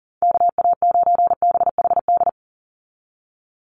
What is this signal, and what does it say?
Morse code, heard as a keyed tone, about 40 wpm, 700 Hz. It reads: KA9BHD
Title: Call Signs Repeated 2x - 40wpm